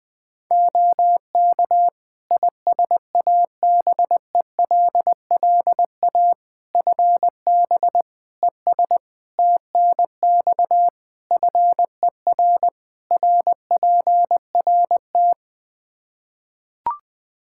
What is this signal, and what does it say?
Morse code, heard as a keyed tone, about 20 wpm, 700 Hz. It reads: OK ISABELLA FB ES TNX FER RPRT
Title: Practice QSO - Exchange 2 - Info - 20wpm